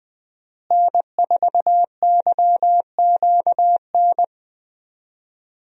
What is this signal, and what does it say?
Morse code, heard as a keyed tone, about 20 wpm, 700 Hz. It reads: N4YQN
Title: USA Call Signs - 5 Characters - Repeated 3x - 20wpm